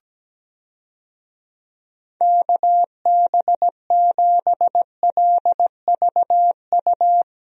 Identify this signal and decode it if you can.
Morse code, heard as a keyed tone, about 17 wpm, 700 Hz. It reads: KB7LVU
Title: USA Call Signs - 6 Characters - Repeated 3x - 17wpm